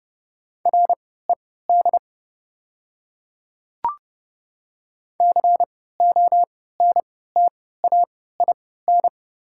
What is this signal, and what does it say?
Morse code, heard as a keyed tone, about 30 wpm, spaced as for 17 wpm, 700 Hz. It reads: RED CONTAIN
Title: 3x Character Spacing - Top 500 Words - 30wpm